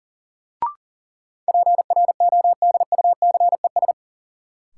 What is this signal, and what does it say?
Morse code, heard as a keyed tone, about 40 wpm, 700 Hz. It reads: PRODUCES
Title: Sentences from Top 1000 Words - Review Part 6 - New Word Drill - 40wpm